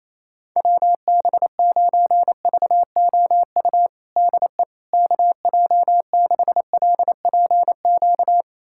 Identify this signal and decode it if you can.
Morse code, heard as a keyed tone, about 27 wpm, 700 Hz. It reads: WB9VOU DE KJ6LPQ
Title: USA Callsign DE Callsign - 6 Characters - 28wpm